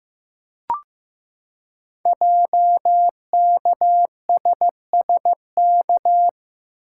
Morse code, heard as a keyed tone, about 15 wpm, 700 Hz. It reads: JKSSK
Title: Code Groups - 5 Letter:Numbers - 15wpm